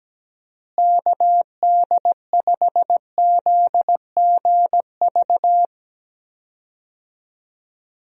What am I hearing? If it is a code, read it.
Morse code, heard as a keyed tone, about 17 wpm, 700 Hz. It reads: KD5ZGV
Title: USA Call Signs - 6 Characters - Repeated 2x - 17wpm